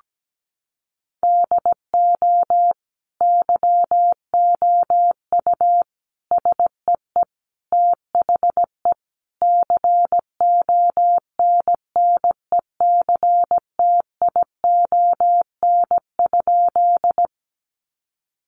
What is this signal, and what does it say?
Morse code, heard as a keyed tone, about 17 wpm, 700 Hz. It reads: DO YOU SEE THE CONNECTION?
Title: Sentences from Top 1300 Words 17wpm